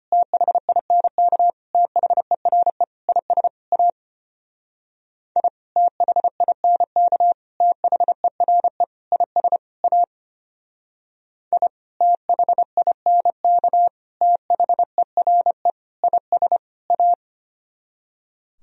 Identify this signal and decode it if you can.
Morse code, heard as a keyed tone, about 31 wpm, 700 Hz. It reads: THINK THERE IS A I THINK THERE IS A I THINK THERE IS A
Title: Speed-Racing - Sets of 5 Words 25wpm